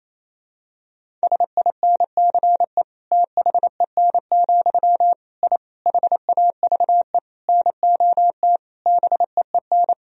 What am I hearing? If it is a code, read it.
Morse code, heard as a keyed tone, about 28 wpm, 700 Hz. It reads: SINCE THEN, I HAVE NOT BEEN
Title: Sentences from Top 1000 Words - Review Part 1 - 28wpm